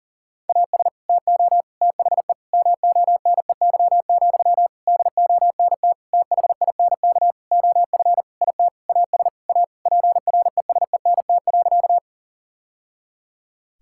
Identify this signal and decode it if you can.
Morse code, heard as a keyed tone, about 40 wpm, 700 Hz. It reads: AS TO THE MONEY, DONT THINK OF IT AS A PRESENT.
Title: Sentences from Top 1000 Words - Review Part 5 - 40wpm